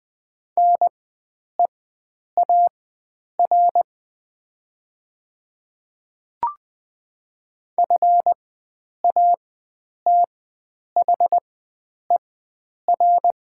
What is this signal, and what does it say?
Morse code, heard as a keyed tone, about 20 wpm, spaced as for 9 wpm, 700 Hz. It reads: NEAR FATHER
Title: 4x Character Spacing - Top 500 Words - 20wpm